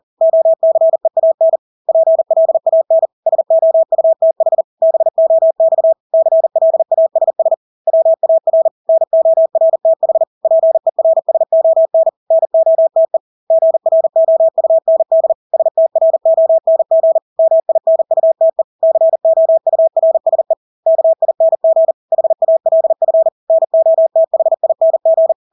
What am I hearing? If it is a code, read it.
Morse code, heard as a keyed tone, about 40 wpm, 650 Hz. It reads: OCEAN PLAN SOUTH BOX CLASS WAR NORTH PERSON NOTE GROUND STRONG MINUTE COURSE KING HALF NOTHING
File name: Engelske ord 40-40 wpm